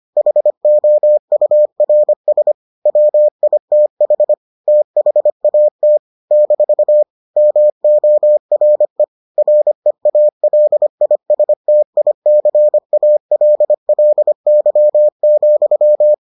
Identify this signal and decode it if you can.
Morse code, heard as a keyed tone, about 25 wpm, 600 Hz. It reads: HOURS WITH THAT - MORE REALISTICALLY,